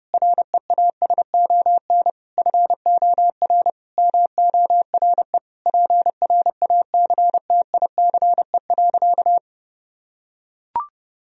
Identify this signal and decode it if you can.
Morse code, heard as a keyed tone, about 30 wpm, 700 Hz. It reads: REASON FOR MORE PRACTICE.
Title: Sentences from Top 1000 Words - Review Part 4 - 30wpm